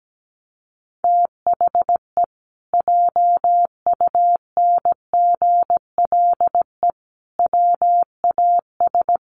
Morse code, heard as a keyed tone, about 17 wpm, 700 Hz. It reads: THE JUNGLE WAS
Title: A Sound Of Thunder 17wpm